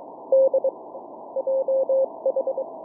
Morse code, heard as a keyed tone, about 22 wpm, 530 Hz. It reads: D JH